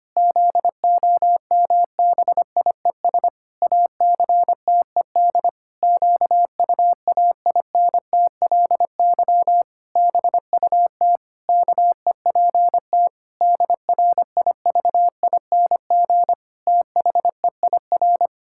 Morse code, heard as a keyed tone, about 25 wpm, 700 Hz. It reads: ZOMBIES ACTED QUAINTLY BUT KEPT DRIVING THEIR
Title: Pangram Challenge v2 - 25wpm